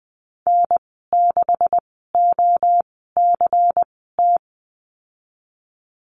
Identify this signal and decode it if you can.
Morse code, heard as a keyed tone, about 20 wpm, spaced as for 15 wpm, 700 Hz. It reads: N6OCT
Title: 147 Lesson 24g - Letter y - Call Signs 20wpm